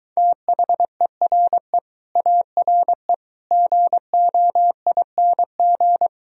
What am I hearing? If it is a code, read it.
Morse code, heard as a keyed tone, about 23 wpm, 700 Hz. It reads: THERE ARE GOING